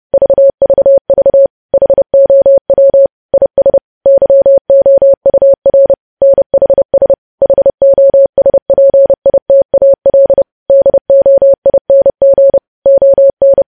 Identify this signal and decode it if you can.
Morse code, heard as a keyed tone, about 30 wpm, 550 Hz. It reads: VVV HOW IS YOUR NHS HOSPITAL DOING ON
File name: podcast / News headlines-1x-30x30